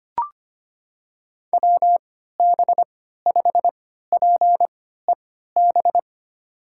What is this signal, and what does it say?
Morse code, heard as a keyed tone, about 25 wpm, spaced as for 14 wpm, 700 Hz. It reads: WB5PEB
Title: USA Call Signs - 6 Characters - 3x Character Spacing - 25wpm